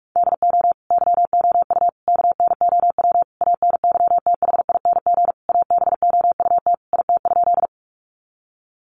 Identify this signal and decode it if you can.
Morse code, heard as a keyed tone, about 44 wpm, 700 Hz. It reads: DO YOU KNOW ANYTHING ABOUT IT?
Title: Sentences from Top 1000 Words 45wpm